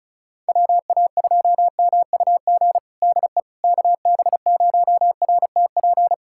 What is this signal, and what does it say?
Morse code, heard as a keyed tone, about 34 wpm, 700 Hz. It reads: WA2MUG DE KB0RTP
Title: USA Callsign DE Callsign - 6 Characters - 35wpm